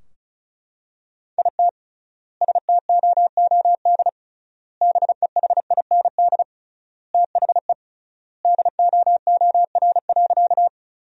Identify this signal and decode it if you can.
Morse code, heard as a keyed tone, about 34 wpm, 700 Hz. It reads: IT STOOD BEHIND THE DOOR.
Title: 3x Word-Spacing - Sentences from Top 500 Words - 35wpm